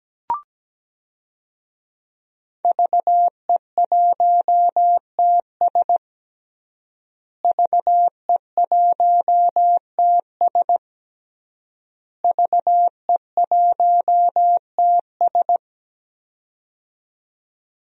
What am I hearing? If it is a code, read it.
Morse code, heard as a keyed tone, about 17 wpm, 700 Hz. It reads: VE1TS VE1TS VE1TS
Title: DXCC-100 Call Signs - 5 Characters - Repeated 3x - 17wpm